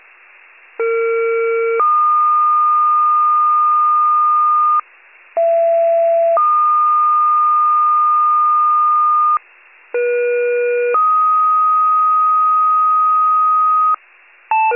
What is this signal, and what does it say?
Two-tone page, heard as a one-second tone, then a three-second tone, about 460 then 1220 Hz, about 670 then 1220 Hz, about 480 then 1250 Hz. It is heard from 0.8 to 13.9 s.
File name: wcalert